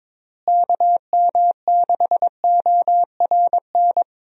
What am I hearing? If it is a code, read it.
Morse code, heard as a keyed tone, about 22 wpm, 700 Hz. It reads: KM6ORN
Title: USA Call Signs - 6 Characters - Repeated 2x - 22wpm